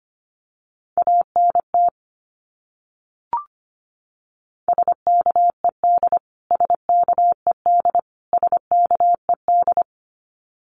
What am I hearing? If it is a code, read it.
Morse code, heard as a keyed tone, about 25 wpm, 700 Hz. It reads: ANT SKED SKED SKED
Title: Abbreviations Repeated 3x - 25wpm